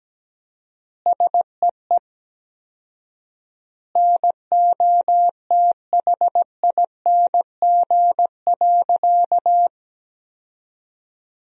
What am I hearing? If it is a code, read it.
Morse code, heard as a keyed tone, about 17 wpm, 700 Hz. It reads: SEE NOTHING.
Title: 4x Word-Spacing - Sentences from Top 500 Words - 17wpm